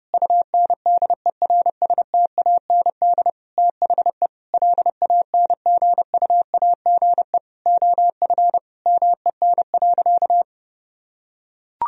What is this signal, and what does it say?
Morse code, heard as a keyed tone, about 30 wpm, 700 Hz. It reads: UNDERSTAND THE LANGUAGE OF MEN.
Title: Sentences from Top 1000 Words - Review Part 3 - 30wpm